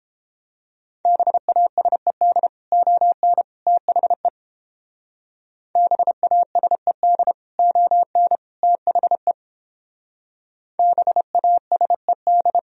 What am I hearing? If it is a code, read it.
Morse code, heard as a keyed tone, about 29 wpm, 700 Hz. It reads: BASED ON THE BASED ON THE BASED